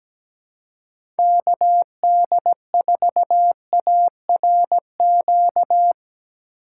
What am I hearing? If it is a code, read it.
Morse code, heard as a keyed tone, about 17 wpm, 700 Hz. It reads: KD4ARQ
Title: USA Call Signs - 6 Characters - 3x Character Spacing - 17wpm